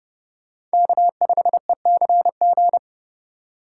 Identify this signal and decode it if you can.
Morse code, heard as a keyed tone, about 31 wpm, 700 Hz. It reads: K5ECG
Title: USA Call Signs - 5 Characters - 4x Character Spacing - 30wpm